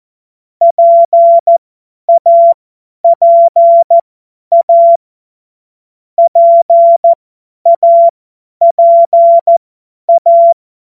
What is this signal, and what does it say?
Morse code, heard as a keyed tone, about 14 wpm, spaced as for 10 wpm, 680 Hz. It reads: PAPA PAPA
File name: Morsekurs / Witze / Witz 02. Februar 2021, 50 F70